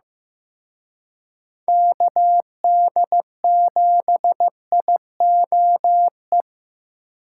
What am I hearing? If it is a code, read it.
Morse code, heard as a keyed tone, about 18 wpm, 700 Hz. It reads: KD7IOE
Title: Speed Racing - USA 2x3 Call Signs 15wpm